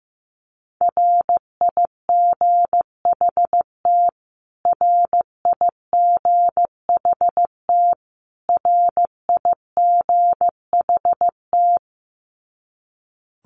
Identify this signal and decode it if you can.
Morse code, heard as a keyed tone, about 15 wpm, 700 Hz. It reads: RIGHT RIGHT RIGHT
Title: Top 200 Words Repeated 15wpm